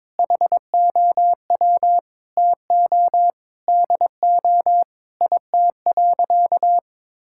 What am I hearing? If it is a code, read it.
Morse code, heard as a keyed tone, about 21 wpm, 700 Hz. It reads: HOW TO DO IT.
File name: practice / 22wpm / Sentences with 8-Word Phrases - 22wpm